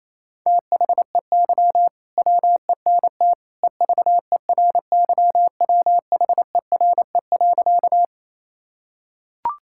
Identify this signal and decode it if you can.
Morse code, heard as a keyed tone, about 28 wpm, 700 Hz. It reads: THEY WENT EVERYWHERE.
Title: Sentences from Top 1600 Words 28wpm